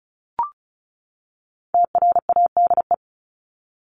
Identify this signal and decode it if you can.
Morse code, heard as a keyed tone, about 34 wpm, 700 Hz. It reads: TRADE
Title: Top 501-750 Words 35wpm